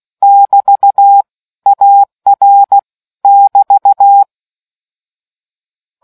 Morse code, heard as a keyed tone, about 16 wpm, 800 Hz. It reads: <BT> AR <BT>